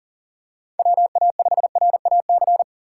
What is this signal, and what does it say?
Morse code, heard as a keyed tone, about 38 wpm, 700 Hz. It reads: WA5RAC
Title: USA Call Signs - 6 Characters - 6x Character Spacing - 40wpm